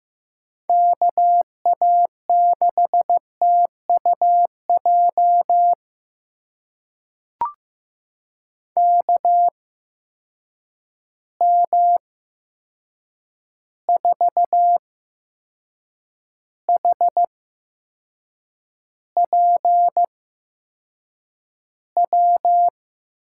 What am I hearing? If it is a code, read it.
Morse code, heard as a keyed tone, about 15 wpm, spaced as for 6 wpm, 700 Hz. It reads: KA6TUJ KM4HPW
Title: USA Call Signs - 6 Characters - 8x Character Spacing - 15wpm